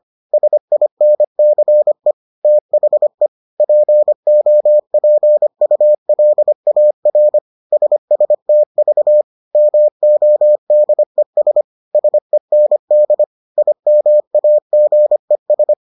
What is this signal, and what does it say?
Morse code, heard as a keyed tone, about 25 wpm, 600 Hz. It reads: SINCE THE POPULAR SSTV MODES SEND IMAGES